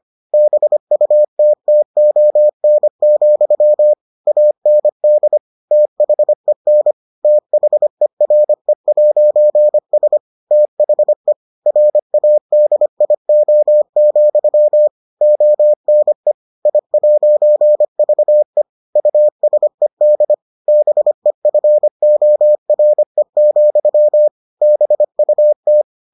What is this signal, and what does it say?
Morse code, heard as a keyed tone, about 25 wpm, 600 Hz. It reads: BUTTON, AND THEN THERE'S THE RADIO, ONE I'VE USED BEFORE, BUT